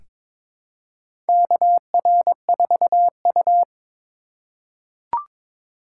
Morse code, heard as a keyed tone, about 22 wpm, 700 Hz. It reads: KR4U
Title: USA Call Signs - 4 Characters - 4x Character Spacing - 22wpm